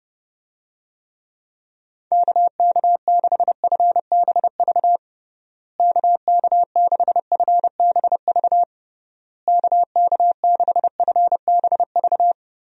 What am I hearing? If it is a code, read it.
Morse code, heard as a keyed tone, about 30 wpm, 700 Hz. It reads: KK6FBV KK6FBV KK6FBV
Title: USA Call Signs - 6 Characters - Repeated 3x - 30wpm